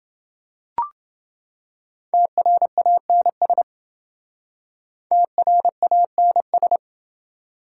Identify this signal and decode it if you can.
Morse code, heard as a keyed tone, about 28 wpm, 700 Hz. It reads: TRANS TRANS